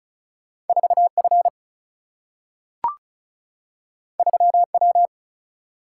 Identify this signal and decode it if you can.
Morse code, heard as a keyed tone, about 36 wpm, 700 Hz. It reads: 4 F 3 W
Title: Single-Letter Single-Number 35wpm